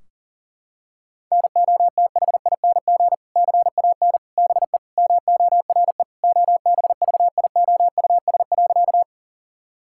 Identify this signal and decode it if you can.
Morse code, heard as a keyed tone, about 37 wpm, 700 Hz. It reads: NOTHING CAN BE MORE OBVIOUS.
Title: Sentences from Top 1600 Words 40wpm